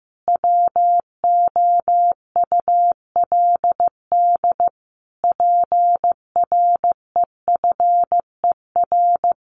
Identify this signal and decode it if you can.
Morse code, heard as a keyed tone, about 15 wpm, 700 Hz. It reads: WOULD PREFER
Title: Sentences from Top 1100 Words 15wpm